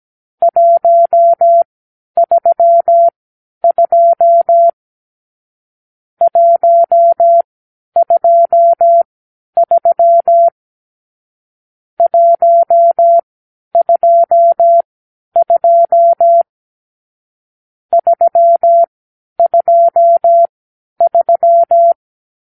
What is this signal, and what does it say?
Morse code, heard as a keyed tone, about 17 wpm, spaced as for 10 wpm, 680 Hz. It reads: 132 123 122 323
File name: k7qo / Lesson 46 (1-4)